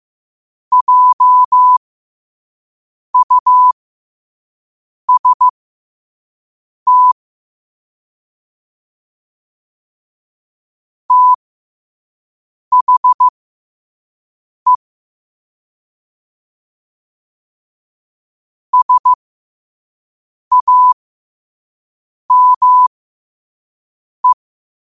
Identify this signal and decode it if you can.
Morse code, heard as a keyed tone, about 15 wpm, spaced as for 5 wpm, 1000 Hz. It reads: JUST THE SAME
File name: qotdfiles / 5wpm